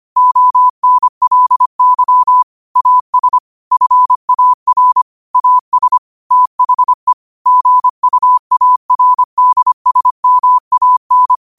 Morse code, heard as a keyed tone, about 25 wpm, 1000 Hz. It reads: ONLY AS FAR AS THE GUARDSMAN